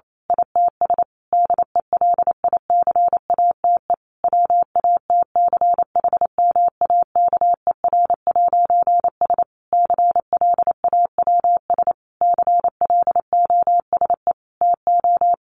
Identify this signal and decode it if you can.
Morse code, heard as a keyed tone, about 28 wpm, 700 Hz. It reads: ITS DELICATE WATCHMAKER'S CLAWS CLOSE TO
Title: A Sound Of Thunder 28wpm